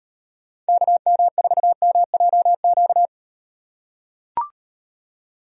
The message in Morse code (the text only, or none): KM4MJQ